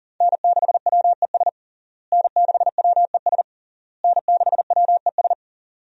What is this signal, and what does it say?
Morse code, heard as a keyed tone, about 40 wpm, 690 Hz. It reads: N6WES N6WES N6WES
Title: USA Call Signs - 5 Characters - Repeated 3x - 40wpm